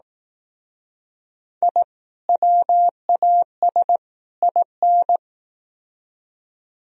Morse code, heard as a keyed tone, about 18 wpm, 700 Hz. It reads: I WAS IN
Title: Speed-Racing - Sets of 3 Words 15wpm